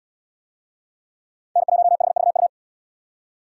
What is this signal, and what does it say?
Morse code, heard as a keyed tone, about 77 wpm, 700 Hz. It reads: N1HVU